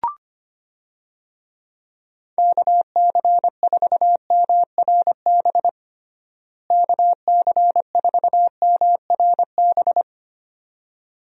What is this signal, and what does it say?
Morse code, heard as a keyed tone, about 25 wpm, 700 Hz. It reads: KC4MRB KC4MRB
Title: USA Call Signs - 6 Characters - Repeated 2x - 25wpm